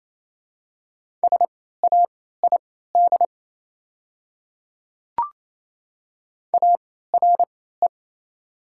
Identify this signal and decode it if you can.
Morse code, heard as a keyed tone, about 27 wpm, spaced as for 16 wpm, 700 Hz. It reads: SAID ARE
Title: 3x Character Spacing - Top 300 Words - 28wpm